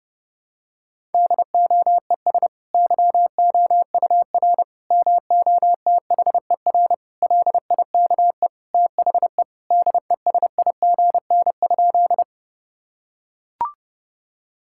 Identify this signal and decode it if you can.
Morse code, heard as a keyed tone, about 30 wpm, 700 Hz. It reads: DOES YOUR MOTHER LIKE THE DESIGN?
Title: Sentences from Top 1000 Words - Review Part 5 - 30wpm